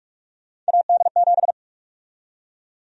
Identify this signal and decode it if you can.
Morse code, heard as a keyed tone, about 47 wpm, 700 Hz. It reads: AD7
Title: USA Call Sign Prefixes - 45wpm